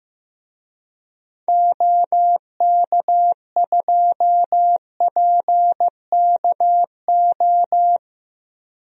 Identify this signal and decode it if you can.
Morse code, heard as a keyed tone, about 15 wpm, 700 Hz. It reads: OK2PKO
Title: DXCC-100 Call Signs - 6 Characters - Repeated 2x - 15wpm